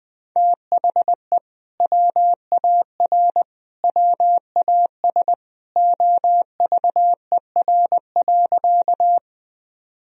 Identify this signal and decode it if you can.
Morse code, heard as a keyed tone, about 20 wpm, 700 Hz. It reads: THE WAR WAS OVER.
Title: Sentences from Top 1000 Words - Review Part 2 - 20wpm